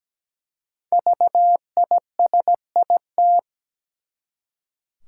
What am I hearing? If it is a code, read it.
Morse code, heard as a keyed tone, about 17 wpm, 700 Hz. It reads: VISIT
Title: Five Letter Words - 17wpm